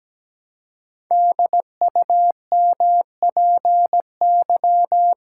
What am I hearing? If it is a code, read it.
Morse code, heard as a keyed tone, about 17 wpm, 700 Hz. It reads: DUMPY
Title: Pangram Challenge v2 - 17wpm